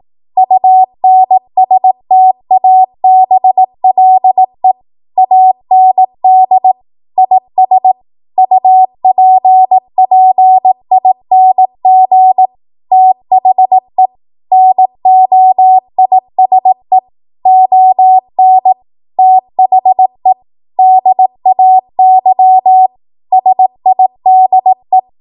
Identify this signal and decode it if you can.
Morse code, heard as a keyed tone, about 18 wpm, 750 Hz. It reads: UNSTABLE AND IS UPPING THE NOISE ON THE DAY SIDE
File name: Archive / 18 WPM / 18WPM